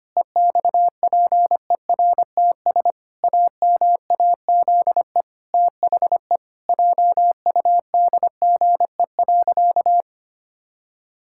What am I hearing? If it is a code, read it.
Morse code, heard as a keyed tone, about 25 wpm, 700 Hz. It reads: EXPERTS AMAZE THE JUDGE.
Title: Pangram Challenge v2 - 25wpm